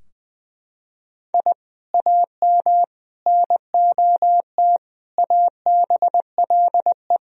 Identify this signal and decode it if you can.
Morse code, heard as a keyed tone, about 21 wpm, 700 Hz. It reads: I AM NOT ABLE
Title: Sentences from Top 1000 Words - Review Part 2 - 20wpm